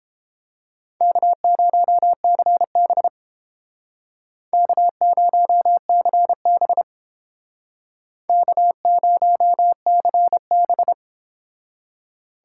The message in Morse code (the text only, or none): K0CB K0CB K0CB